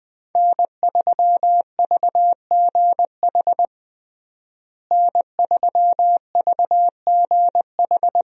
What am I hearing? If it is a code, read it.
Morse code, heard as a keyed tone, about 20 wpm, 700 Hz. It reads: N3VGH N3VGH
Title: USA Call Signs - 5 Characters - Repeated 3x - 20wpm